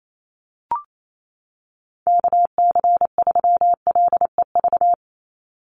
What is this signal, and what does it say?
Morse code, heard as a keyed tone, about 28 wpm, 700 Hz. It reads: KC3LEV
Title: USA Call Signs 28wpm